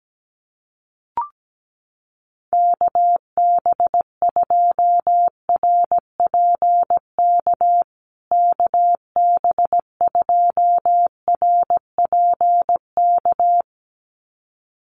Morse code, heard as a keyed tone, about 17 wpm, 700 Hz. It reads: KB2RPK KB2RPK
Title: USA Call Signs Repeated 2x 17wpm